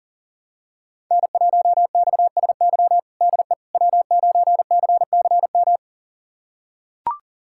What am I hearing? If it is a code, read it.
Morse code, heard as a keyed tone, about 40 wpm, 700 Hz. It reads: N1XSY DE W9CCM
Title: USA Callsign DE Callsign - 5 Characters - 40wpm